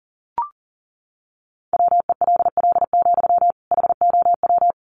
Morse code, heard as a keyed tone, about 40 wpm, 700 Hz. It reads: WELL, HOW
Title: Sentences from Top 1200 Words 40wpm